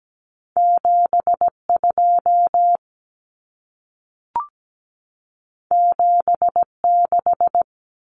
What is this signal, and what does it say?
Morse code, heard as a keyed tone, about 17 wpm, 700 Hz. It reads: 72 76
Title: Common Cut Numbers - 2-Digit 17wpm